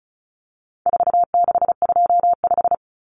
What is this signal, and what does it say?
Morse code, heard as a keyed tone, about 34 wpm, 700 Hz. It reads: 4625
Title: Common Cut Numbers - 4-Digit 35wpm